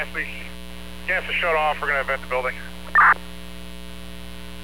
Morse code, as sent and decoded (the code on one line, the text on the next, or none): none